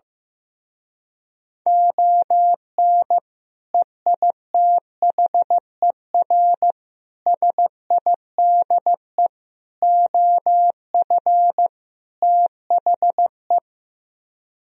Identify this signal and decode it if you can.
Morse code, heard as a keyed tone, about 15 wpm, 700 Hz. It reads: ON EITHER SIDE OF THE
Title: Speed-Racing - Sets of 5 Words 15wpm